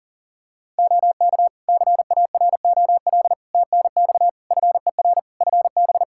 Morse code, heard as a keyed tone, about 40 wpm, 700 Hz. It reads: OK CAROL TNX FER FB
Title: K1USN SST QSO Practice 40wpm